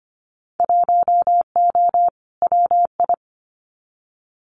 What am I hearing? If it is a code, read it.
Morse code, heard as a keyed tone, about 25 wpm, 700 Hz. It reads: 1O WI
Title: Winter Field Day Exchanges 25wpm